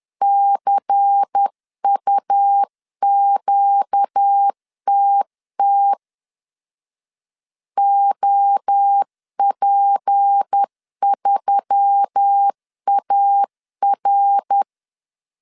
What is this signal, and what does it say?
Morse code, heard as a keyed tone, about 10 wpm, 800 Hz. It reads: CUQTT OP3AR
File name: CW / DET / 1 6 1500C S C 10wpm 4252